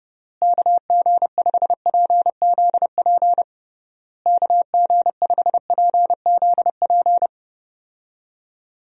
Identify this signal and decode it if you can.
Morse code, heard as a keyed tone, about 29 wpm, 700 Hz. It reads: KG5PZP KG5PZP
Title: USA Call Signs - 6 Characters - Repeated 3x - 30wpm